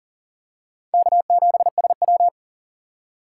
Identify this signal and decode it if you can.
Morse code, heard as a keyed tone, about 40 wpm, 700 Hz. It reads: K7SW
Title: USA Call Signs - 4 Characters - 3x Character Spacing - 40wpm